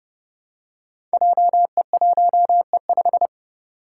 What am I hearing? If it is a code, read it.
Morse code, heard as a keyed tone, about 30 wpm, 700 Hz. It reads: JE1E5
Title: Five Challenging Letter-Number Combinations 30wpm